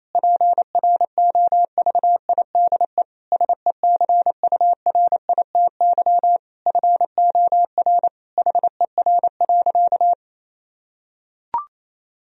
Morse code, heard as a keyed tone, about 28 wpm, 700 Hz. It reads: PROVIDE SECURITY FOR HER.
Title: Sentences from Top 1000 Words - Review Part 4 - 28wpm